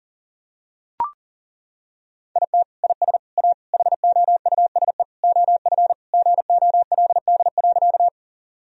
Morse code, heard as a keyed tone, about 40 wpm, 700 Hz. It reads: IT IS A HOUSE OF GOLD.
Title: Sentences from Top 1000 Words - Review Part 3 - 40wpm